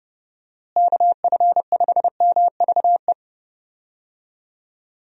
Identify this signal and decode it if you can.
Morse code, heard as a keyed tone, about 30 wpm, 700 Hz. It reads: KF5MVE
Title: USA Call Signs - 6 Characters - Repeated 3x - 30wpm